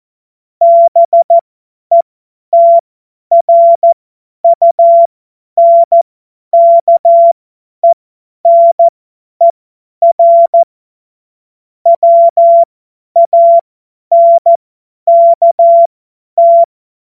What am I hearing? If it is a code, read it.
Morse code, heard as a keyed tone, about 14 wpm, spaced as for 10 wpm, 680 Hz. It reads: BETRUNKENER WANKT
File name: Morsekurs / Witze / Witz 01. Juni 2021, 50 F70